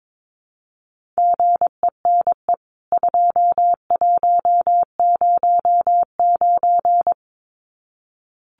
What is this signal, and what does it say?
Morse code, heard as a keyed tone, about 22 wpm, 700 Hz. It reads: GENE 2109
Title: CWT Exchange 22wpm